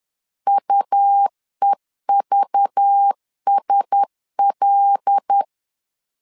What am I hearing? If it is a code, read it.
Morse code, heard as a keyed tone, about 10 wpm, 800 Hz. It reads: UEVSL
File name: LEZIONI / 5 LBVC / 5 LBV 10wpm 2248